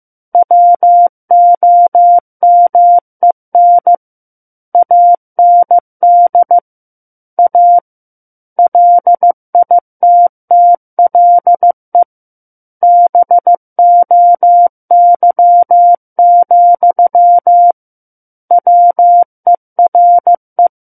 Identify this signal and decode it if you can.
Morse code, heard as a keyed tone, about 15 wpm, 700 Hz. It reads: WOMEN AND A LITTLE BOY, WERE
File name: Chapter 6 - 15 WPM